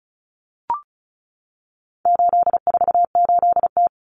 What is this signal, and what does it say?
Morse code, heard as a keyed tone, about 34 wpm, 700 Hz. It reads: 848T
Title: Common Cut Numbers - 4-Digit 35wpm